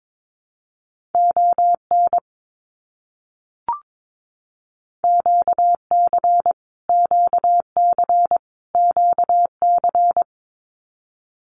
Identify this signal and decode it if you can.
Morse code, heard as a keyed tone, about 22 wpm, 700 Hz. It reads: ON QC QC QC
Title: Canadian Provinces and Territories - Repeated 3x - 22wpm